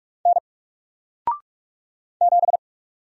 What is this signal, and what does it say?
Morse code, heard as a keyed tone, about 44 wpm, 700 Hz. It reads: N 7
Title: Single Character (Letter:Number:Punctuation:Prosign) Alt 45wpm